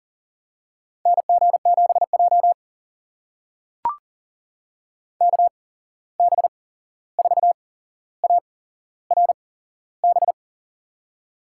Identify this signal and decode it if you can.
Morse code, heard as a keyed tone, about 40 wpm, spaced as for 15 wpm, 700 Hz. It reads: NG7J KB4ARB
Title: USA Call Signs - 8x Character Spacing - 40wpm